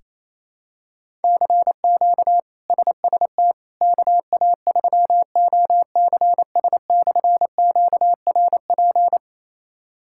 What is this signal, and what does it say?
Morse code, heard as a keyed tone, about 28 wpm, 700 Hz. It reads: CQ SST KA3OCS/QRP
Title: K1USN SST QSO Practice - 28wpm